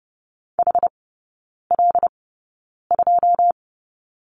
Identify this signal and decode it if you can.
Morse code, heard as a keyed tone, about 30 wpm, spaced as for 9 wpm, 700 Hz. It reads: HL2
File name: 081 Lesson 14e - Number 2 - Three Characters 30wpm